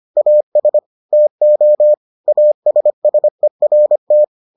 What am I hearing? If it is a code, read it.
Morse code, heard as a keyed tone, about 25 wpm, 600 Hz. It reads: AS TO ASSERT